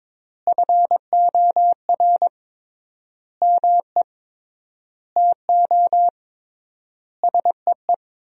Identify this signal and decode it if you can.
Morse code, heard as a keyed tone, about 22 wpm, 700 Hz. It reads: FOR ME TO SEE
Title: 3x Word-Spacing - Sentences from Top 500 Words - 22wpm